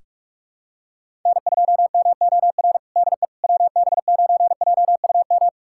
Morse code, heard as a keyed tone, about 48 wpm, 700 Hz. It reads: N2MOR DE WB9JUM
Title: USA Callsign DE Callsign 45wpm